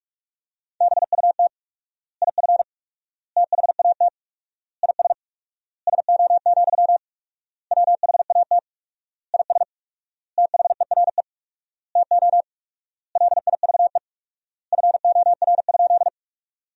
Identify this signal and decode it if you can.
Morse code, heard as a keyed tone, about 42 wpm, 700 Hz. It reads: BUT IF THAT IS SO, WHAT IS THERE TO LIVE FOR?
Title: 4x Word-Spacing - Sentences from Top 200 Words - 45wpm